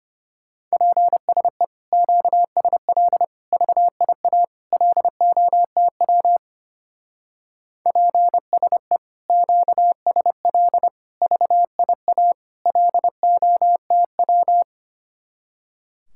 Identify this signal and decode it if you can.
Morse code, heard as a keyed tone, about 31 wpm, 700 Hz. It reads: PSE QSL VIA LOTW PSE QSL VIA LOTW
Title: Speed-Racing Common QSO Phrases Encore - 25wpm